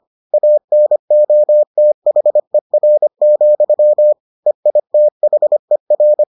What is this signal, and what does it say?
Morse code, heard as a keyed tone, about 25 wpm, 600 Hz. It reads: ANOTHER, EITHER